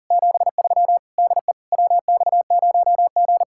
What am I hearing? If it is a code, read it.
Morse code, heard as a keyed tone, about 39 wpm, 700 Hz. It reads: 73 DE WX0G
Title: Speed-Racing Common QSO Phrases - 40wpm